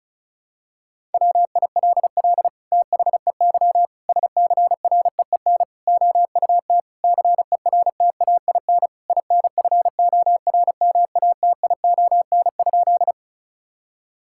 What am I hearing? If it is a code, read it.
Morse code, heard as a keyed tone, about 36 wpm, 700 Hz. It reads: WILL THEY SCREEN OUT CERTAIN INFORMATION?
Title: Sentences from Top 1000 Words - Review Part 6 - 35wpm